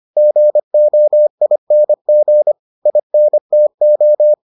Morse code, heard as a keyed tone, about 25 wpm, 600 Hz. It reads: GOING INTO